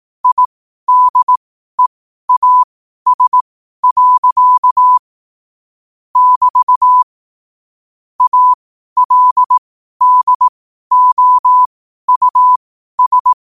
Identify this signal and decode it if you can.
Morse code, heard as a keyed tone, about 18 wpm, spaced as for 12 wpm, 1000 Hz. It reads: IDEAS. <BT> ALDOUS
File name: qotdfiles / 12wpm